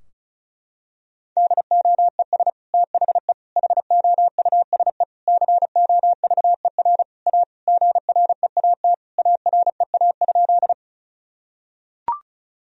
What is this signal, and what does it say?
Morse code, heard as a keyed tone, about 34 wpm, 700 Hz. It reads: DOES THE HOUSE COVER A GREAT AREA?
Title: Sentences from Top 1000 Words - Review Part 2 - 35wpm